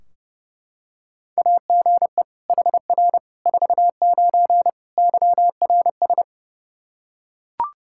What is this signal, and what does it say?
Morse code, heard as a keyed tone, about 30 wpm, 700 Hz. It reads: AGE HR 49 YRS
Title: Practice QSO - Exchange 3 - More Info - 30wpm